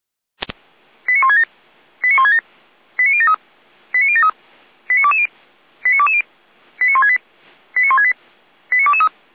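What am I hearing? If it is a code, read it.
Five-tone selective call (ZVEI), heard as quick runs of five tones, tones of about 70 ms. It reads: 89167 89167 89073 89073 89220 89220 89168 89168 89223
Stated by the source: ZVEI five-tone selective call 89011 received via SDR.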